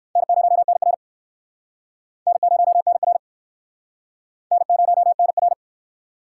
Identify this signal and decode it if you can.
Morse code, heard as a keyed tone, about 69 wpm, 700 Hz. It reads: N0NR N0NR N0NR